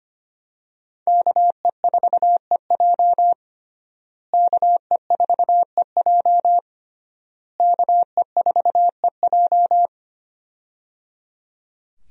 Morse code, heard as a keyed tone, about 25 wpm, 700 Hz. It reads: KE4EJ KE4EJ KE4EJ
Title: USA Call Signs - 5 Characters - Repeated 3x - 25wpm